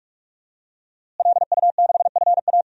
Morse code, heard as a keyed tone, about 46 wpm, 700 Hz. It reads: LU6FA
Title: DXCC-100 Call Signs - 5 Characters - Repeated 2x - 45wpm